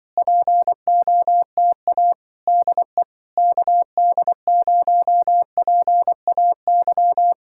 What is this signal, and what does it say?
Morse code, heard as a keyed tone, about 24 wpm, 700 Hz. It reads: POTA DE KD0PAY
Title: Speed-Racing Common QSO Phrases - 20wpm